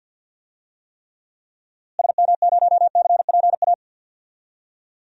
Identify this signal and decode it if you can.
Morse code, heard as a keyed tone, about 47 wpm, 700 Hz. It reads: SM0CPA
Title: DXCC-100 Call Signs - 6 Characters - Repeated 3x - 50wpm